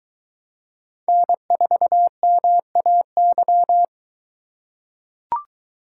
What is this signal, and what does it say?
Morse code, heard as a keyed tone, about 23 wpm, 700 Hz. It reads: N4MAY